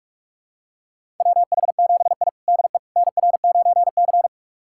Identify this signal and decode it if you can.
Morse code, heard as a keyed tone, about 46 wpm, 700 Hz. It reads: WH7I DE NR9C
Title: USA Callsign DE Callsign - 4 Characters - 45wpm